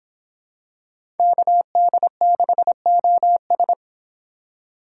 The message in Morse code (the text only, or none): KD6OS